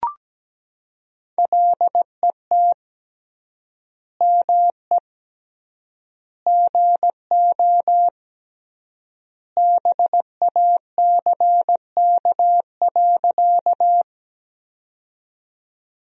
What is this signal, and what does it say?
Morse code, heard as a keyed tone, about 17 wpm, 700 Hz. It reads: LET ME GO BACK.
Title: 3x Word-Spacing - Sentences from Top 300 Words - 17wpm